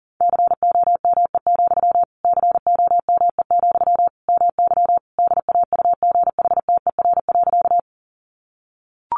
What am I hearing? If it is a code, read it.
Morse code, heard as a keyed tone, about 40 wpm, 700 Hz. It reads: COME, COME, MY DAUGHTER.
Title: Sentences from Top 1500 Words 40wpm